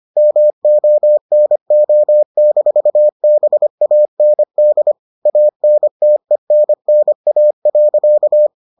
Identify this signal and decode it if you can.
Morse code, heard as a keyed tone, about 25 wpm, 600 Hz. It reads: MONO-BAND ANTENNA.